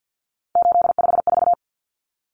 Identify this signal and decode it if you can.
Morse code, heard as a keyed tone, about 50 wpm, 700 Hz. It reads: 854
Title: Common Cut Numbers - 3-Digit 50wpm